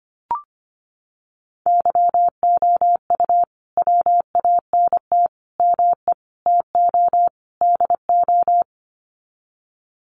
Morse code, heard as a keyed tone, about 25 wpm, 700 Hz. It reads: YOU WANT ME TO DO
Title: Sets of 5 Words Encore2 25wpm